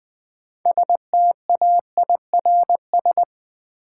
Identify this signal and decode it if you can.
Morse code, heard as a keyed tone, about 20 wpm, 700 Hz. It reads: STAIRS
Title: Sets of 5 Words Encore - New Word Drill - 20wpm